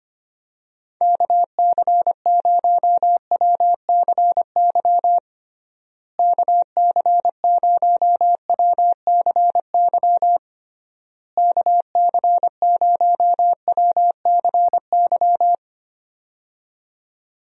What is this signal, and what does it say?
Morse code, heard as a keyed tone, about 25 wpm, 700 Hz. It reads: KC0WCY KC0WCY KC0WCY
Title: USA Call Signs - 6 Characters - Repeated 3x - 25wpm